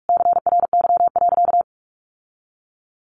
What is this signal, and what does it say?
Morse code, heard as a keyed tone, about 44 wpm, 700 Hz. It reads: CRY.
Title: Sentences from Top 1400 Words 45wpm